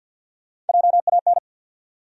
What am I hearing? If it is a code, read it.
Morse code, heard as a keyed tone, about 54 wpm, 700 Hz. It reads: JAN
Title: USA Common Names - Max 3 Letters - 50wpm